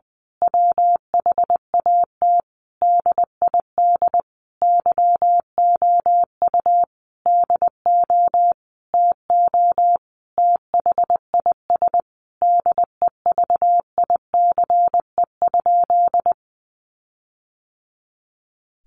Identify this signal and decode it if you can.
Morse code, heard as a keyed tone, about 20 wpm, 700 Hz. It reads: WHAT DID YOU DO TO THIS DEVICE?
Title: Sentences from Top 1500 Words 20wpm